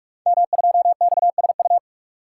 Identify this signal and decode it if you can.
Morse code, heard as a keyed tone, about 44 wpm, 700 Hz. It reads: M2XSU
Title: Code Groups - 5 Letter:Numbers - 45wpm